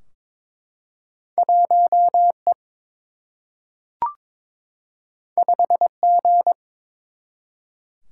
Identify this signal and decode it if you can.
Morse code, heard as a keyed tone, about 23 wpm, 700 Hz. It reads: 1 E 5 G
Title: Single-Letter Single-Number 22wpm